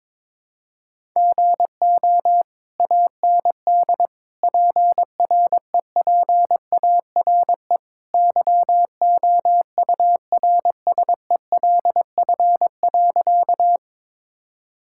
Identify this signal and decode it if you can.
Morse code, heard as a keyed tone, about 22 wpm, 700 Hz. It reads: GO AND PREPARE YOURSELF.
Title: Sentences from Top 1000 Words - Review Part 6 - 22wpm